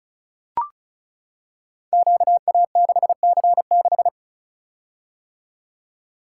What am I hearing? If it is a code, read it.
Morse code, heard as a keyed tone, about 36 wpm, 700 Hz. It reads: QA6C6
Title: Code Groups - 5 Letter:Numbers - 35wpm